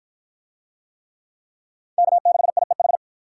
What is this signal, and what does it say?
Morse code, heard as a keyed tone, about 48 wpm, 700 Hz. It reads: K6IEH